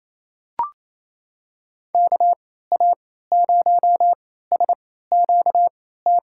Morse code, heard as a keyed tone, about 28 wpm, spaced as for 16 wpm, 700 Hz. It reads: KA0SQT
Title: USA Call Signs - 6 Characters - 3x Character Spacing - 28wpm